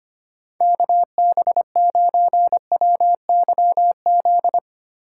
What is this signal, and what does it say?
Morse code, heard as a keyed tone, about 25 wpm, 700 Hz. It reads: KB9WYZ
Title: USA Call Signs - 6 Characters - 4x Character Spacing - 25wpm